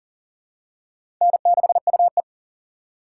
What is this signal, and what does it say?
Morse code, heard as a keyed tone, about 37 wpm, 700 Hz. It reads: N6UE
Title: USA Call Signs - 4 Characters - 3x Character Spacing - 40wpm